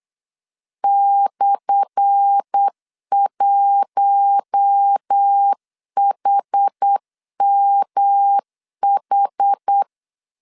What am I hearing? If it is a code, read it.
Morse code, heard as a keyed tone, about 8 wpm, 800 Hz. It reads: /1HMH
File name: LEZIONI / 10 SIGNS / 10A SLASH 8wpm 3452